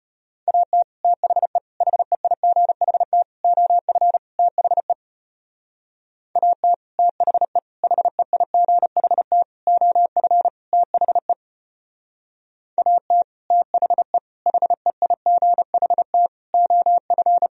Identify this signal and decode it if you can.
Morse code, heard as a keyed tone, about 31 wpm, 700 Hz. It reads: AT THE HEIGHT OF THE AT THE HEIGHT OF THE AT THE HEIGHT OF
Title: Speed-Racing - Sets of 5 Words 25wpm